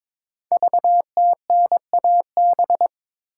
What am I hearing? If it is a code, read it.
Morse code, heard as a keyed tone, about 23 wpm, 700 Hz. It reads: VTNAB
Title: Code Groups - 5 Letters - 22wpm